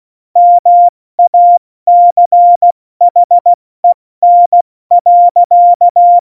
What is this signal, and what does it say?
Morse code, heard as a keyed tone, about 16 wpm, spaced as for 14 wpm, 700 Hz. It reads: MACHEN.
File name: Morsekurs / Witze / Witz 04. Juni 2019, 70 F80